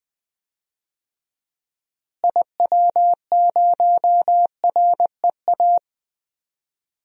Morse code, heard as a keyed tone, about 20 wpm, 700 Hz. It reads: IW0REA
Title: DXCC-100 Call Signs - 6 Characters - Repeated 2x - 20wpm